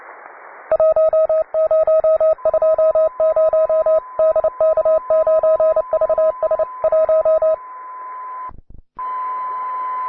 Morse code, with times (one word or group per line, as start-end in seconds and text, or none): 0.7-4.0 1020
4.2-6.6 DK9VS
6.8-7.5 1